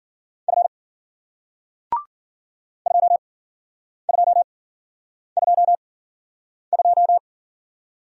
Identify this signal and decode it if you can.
Morse code, heard as a keyed tone, about 53 wpm, spaced as for 9 wpm, 700 Hz. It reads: F 2222